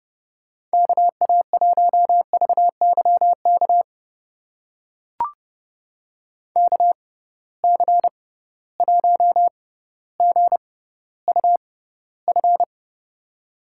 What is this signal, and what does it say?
Morse code, heard as a keyed tone, about 30 wpm, spaced as for 15 wpm, 700 Hz. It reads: KA1VYK KC1GUF
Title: USA Call Signs - 6 Characters - 6x Character Spacing - 30wpm